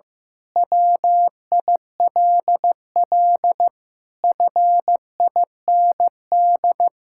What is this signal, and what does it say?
Morse code, heard as a keyed tone, about 15 wpm, 700 Hz. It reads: WILL FIND
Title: Sentences from Top 1000 Words - Review Part 6 - 15wpm